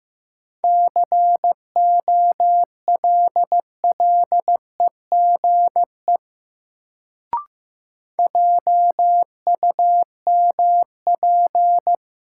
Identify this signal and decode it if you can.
Morse code, heard as a keyed tone, about 15 wpm, 700 Hz. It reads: COLLEGE JUMP
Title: Sentences from Top 1300 Words - New Word Drill - 15wpm